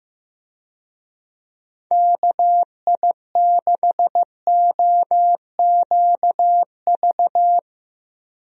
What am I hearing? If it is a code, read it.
Morse code, heard as a keyed tone, about 15 wpm, 700 Hz. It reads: KI6OQV
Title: USA Call Signs - 6 Characters - Repeated 2x - 15wpm